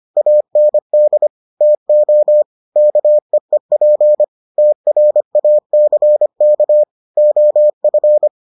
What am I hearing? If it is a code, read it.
Morse code, heard as a keyed tone, about 25 wpm, 600 Hz. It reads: AND TO KEEP TRACK OF